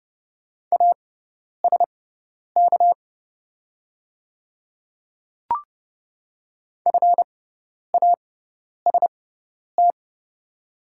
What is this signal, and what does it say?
Morse code, heard as a keyed tone, about 30 wpm, spaced as for 10 wpm, 700 Hz. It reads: ASK FAST
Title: 6x Character Spacing - Top 500 Words - 30wpm